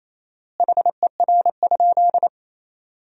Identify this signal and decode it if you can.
Morse code, heard as a keyed tone, about 28 wpm, 700 Hz. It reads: HER?
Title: 4x Word-Spacing - Sentences from Top 200 Words - 28wpm